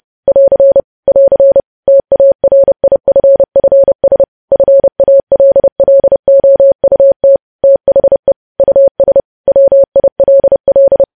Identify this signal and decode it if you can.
Morse code, heard as a keyed tone, about 30 wpm, 550 Hz. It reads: <AR> <AR> TARIFFS FALLOUT THE US WILL
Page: news headlines in morse code at 30 wpm